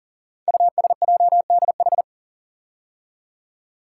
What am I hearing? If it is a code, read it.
Morse code, heard as a keyed tone, about 42 wpm, 700 Hz. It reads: USJDH
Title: Code Groups - 5 Letters - 40wpm